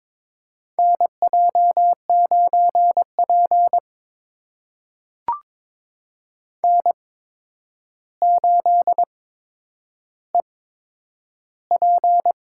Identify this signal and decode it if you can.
Morse code, heard as a keyed tone, about 22 wpm, spaced as for 9 wpm, 700 Hz. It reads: NJ9P N8EP
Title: USA Call Signs - 4 Characters - 8x Character Spacing - 22wpm